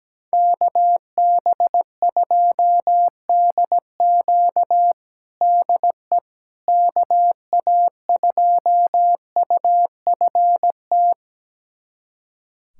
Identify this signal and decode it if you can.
Morse code, heard as a keyed tone, about 17 wpm, 700 Hz. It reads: KB2DQ DE KA2UFT
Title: USA Callsign DE Callsign 17wpm